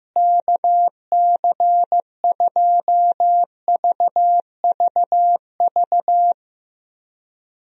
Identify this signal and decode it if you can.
Morse code, heard as a keyed tone, about 16 wpm, 700 Hz. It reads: KC2VVV
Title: USA Call Signs - 6 Characters - Repeated 2x - 15wpm